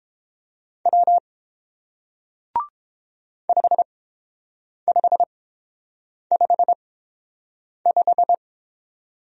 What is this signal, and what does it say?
Morse code, heard as a keyed tone, about 30 wpm, spaced as for 7 wpm, 700 Hz. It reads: W 5555